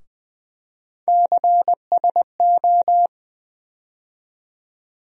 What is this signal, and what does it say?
Morse code, heard as a keyed tone, about 20 wpm, 700 Hz. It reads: CSO
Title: USA Call Sign Postfixes - 3 Letters 20wpm